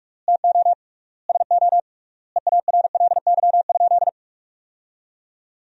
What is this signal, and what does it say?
Morse code, heard as a keyed tone, about 46 wpm, 700 Hz. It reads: TO SO EARLY?
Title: 3x Word-Spacing - Sentences from Top 400 Words - 45wpm